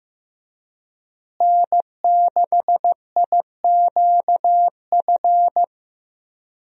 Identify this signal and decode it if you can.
Morse code, heard as a keyed tone, about 15 wpm, 700 Hz. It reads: N6IQF
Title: USA Call Signs - 5 Characters - Repeated 3x - 15wpm